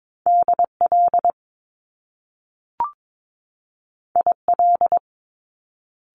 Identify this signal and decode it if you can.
Morse code, heard as a keyed tone, about 22 wpm, 700 Hz. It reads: DL IL
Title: Top Two Letter Combinations in Words 22wpm